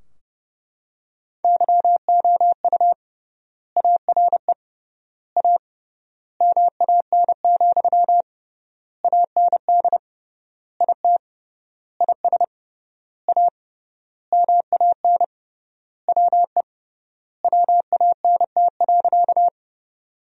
Morse code, heard as a keyed tone, about 31 wpm, 700 Hz. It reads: YOU ARE A MAN, AND IT IS A MAN WE WANT.
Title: 3x Word-Spacing - Sentences from Top 200 Words - 30wpm